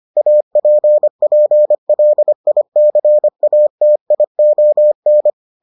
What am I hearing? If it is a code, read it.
Morse code, heard as a keyed tone, about 25 wpm, 600 Hz. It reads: APPLICATION